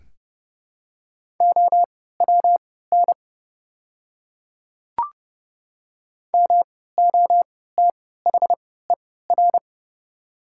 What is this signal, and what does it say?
Morse code, heard as a keyed tone, about 29 wpm, spaced as for 17 wpm, 690 Hz. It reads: OWN MOTHER
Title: 3x Character Spacing - Top 300 Words - 30wpm